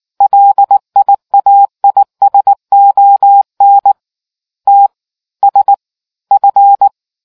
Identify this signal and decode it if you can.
Morse code, heard as a keyed tone, about 19 wpm, spaced as for 16 wpm, 800 Hz. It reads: LIAISON TSF